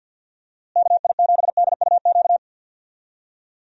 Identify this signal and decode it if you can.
Morse code, heard as a keyed tone, about 52 wpm, 700 Hz. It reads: KI7DUQ
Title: USA Call Signs - 6 Characters - Repeated 3x - 50wpm